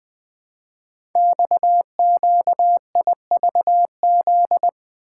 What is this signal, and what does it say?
Morse code, heard as a keyed tone, about 20 wpm, 700 Hz. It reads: XQIVZ
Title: Code Groups - 5 Letters - 20wpm